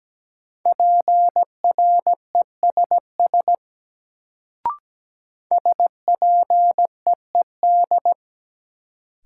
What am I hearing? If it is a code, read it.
Morse code, heard as a keyed tone, about 17 wpm, 700 Hz. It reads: PRESS SPEED
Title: Sentences from Top 800 Words - New Word Drill - 17wpm